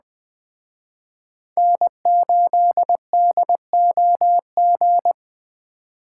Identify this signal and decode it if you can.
Morse code, heard as a keyed tone, about 20 wpm, 700 Hz. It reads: N8DOG
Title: Speed Racing - USA Call Signs 17wpm